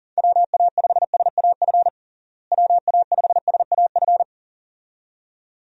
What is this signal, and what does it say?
Morse code, heard as a keyed tone, about 40 wpm, 700 Hz. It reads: WA5SAF WA5SAF
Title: USA Call Signs - 6 Characters - Repeated 2x - 40wpm